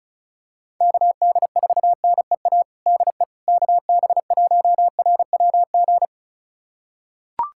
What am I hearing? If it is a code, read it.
Morse code, heard as a keyed tone, about 36 wpm, 700 Hz. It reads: KD4NEA DE KB1RWG
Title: USA Callsign DE Callsign - 6 Characters - 35wpm